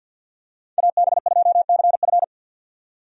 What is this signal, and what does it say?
Morse code, heard as a keyed tone, about 49 wpm, 700 Hz. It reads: AB2CF
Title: USA Call Signs - 5 Characters - 50wpm